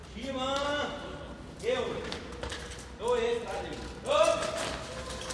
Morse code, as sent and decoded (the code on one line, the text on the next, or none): none